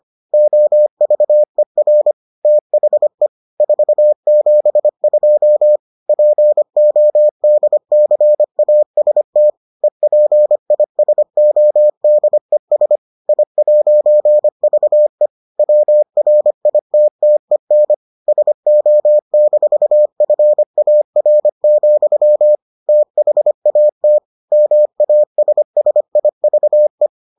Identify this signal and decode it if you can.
Morse code, heard as a keyed tone, about 25 wpm, 600 Hz. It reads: OVER THE 472 PODCAST EPISODES I'VE WRITTEN SO-FAR, THAT MASSIVE